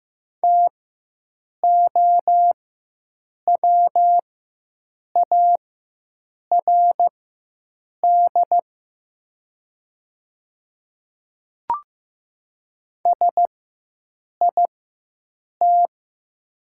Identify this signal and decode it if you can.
Morse code, heard as a keyed tone, about 15 wpm, spaced as for 7 wpm, 700 Hz. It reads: TOWARD SIT
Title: 4x Character Spacing - Top 500 Words - 15wpm